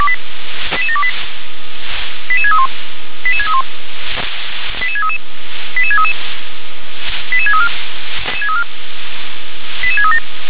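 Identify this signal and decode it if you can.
Five-tone selective call (ZVEI), heard as quick runs of five tones, tones of about 70 ms. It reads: 88728 88631 88631 88633 88633 88634 88637
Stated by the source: Five-tone ZVEI selective call received by SDR.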